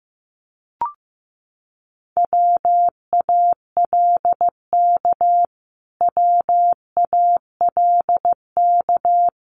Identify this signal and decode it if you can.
Morse code, heard as a keyed tone, about 15 wpm, 700 Hz. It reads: WALK WALK
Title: Top 500 Words Repeated 15wpm